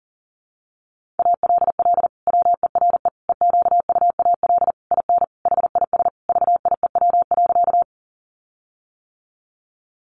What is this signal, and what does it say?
Morse code, heard as a keyed tone, about 40 wpm, 700 Hz. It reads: ALL WERE EQUAL IN HIS VIEW.
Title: Sentences from Top 1400 Words 40wpm